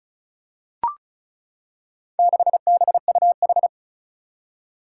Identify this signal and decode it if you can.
Morse code, heard as a keyed tone, about 36 wpm, 700 Hz. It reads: 6BUH
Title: Four Challenging Letter-Number Combinations 35wpm